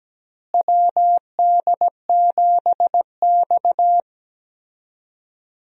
Morse code, heard as a keyed tone, about 17 wpm, 700 Hz. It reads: WD7X
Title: USA Call Signs - 4 Characters - Repeated 2x - 17wpm